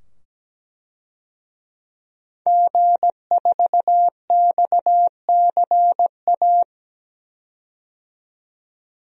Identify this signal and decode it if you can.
Morse code, heard as a keyed tone, about 17 wpm, 700 Hz. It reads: G4XCA
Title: DXCC-100 Call Signs - 5 Characters - 17wpm